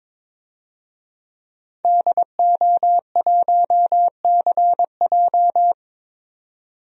Morse code, heard as a keyed tone, about 21 wpm, 700 Hz. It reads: DO1CJ
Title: DXCC-100 Call Signs - 5 Characters - 22wpm